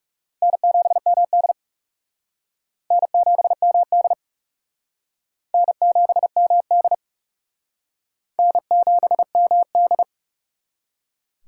Morse code, heard as a keyed tone, about 38 wpm, 700 Hz. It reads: N7MD N7MD N7MD N7MD